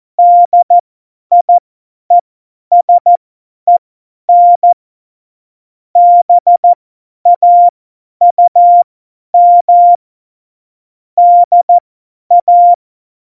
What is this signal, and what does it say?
Morse code, heard as a keyed tone, about 14 wpm, spaced as for 10 wpm, 700 Hz. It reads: DIESEN BAUM DA